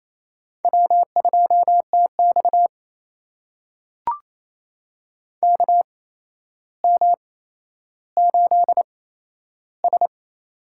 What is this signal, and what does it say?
Morse code, heard as a keyed tone, about 27 wpm, spaced as for 12 wpm, 700 Hz. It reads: W2TX KM8S
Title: USA Call Signs - 4 Characters - 8x Character Spacing - 28wpm